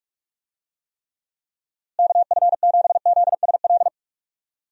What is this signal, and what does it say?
Morse code, heard as a keyed tone, about 43 wpm, 700 Hz. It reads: KF7ZSL
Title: USA Call Signs - 6 Characters - Repeated 2x - 45wpm